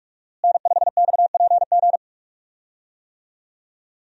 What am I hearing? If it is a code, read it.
Morse code, heard as a keyed tone, about 47 wpm, 700 Hz. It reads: N5XPG
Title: USA Call Signs - 5 Characters - Repeated 3x - 45wpm